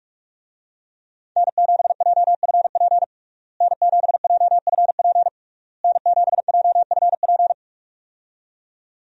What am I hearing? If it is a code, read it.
Morse code, heard as a keyed tone, about 46 wpm, 700 Hz. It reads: N7JFP N7JFP N7JFP
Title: USA Call Signs - 5 Characters - Repeated 3x - 45wpm